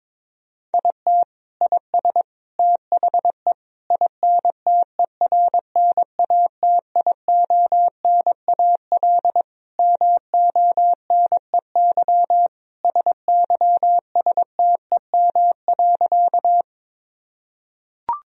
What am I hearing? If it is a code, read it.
Morse code, heard as a keyed tone, about 21 wpm, 700 Hz. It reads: IT IS THE INTERNATIONAL MONEY SYSTEM.
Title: Sentences from Top 1000 Words - Review Part 4 - 22wpm